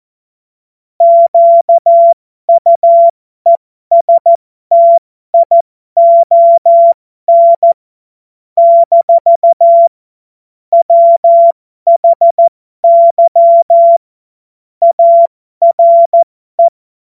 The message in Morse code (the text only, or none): QUESTION - WHY ARE